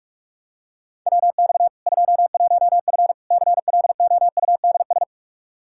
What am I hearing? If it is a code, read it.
Morse code, heard as a keyed tone, about 43 wpm, 700 Hz. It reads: WX 21F CLOUDS
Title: K1USN SST QSO Practice 45wpm